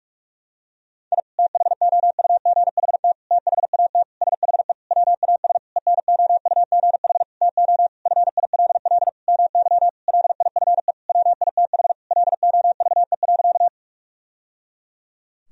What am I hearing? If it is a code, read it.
Morse code, heard as a keyed tone, about 46 wpm, 700 Hz. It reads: I THOUGHT THAT SHE WAS ENOUGH TO FILL MY LIFE WITH LOVE.
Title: Sentences from Top 1000 Words - Review Part 3 - 45wpm